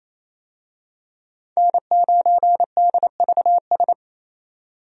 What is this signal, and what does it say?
Morse code, heard as a keyed tone, about 28 wpm, 700 Hz. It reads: N9DVS
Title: USA Call Signs - 5 Characters - 28wpm